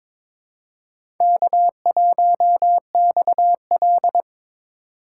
Morse code, heard as a keyed tone, about 23 wpm, 700 Hz. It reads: K1XL
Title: USA Call Signs - 4 Characters - 8x Character Spacing - 22wpm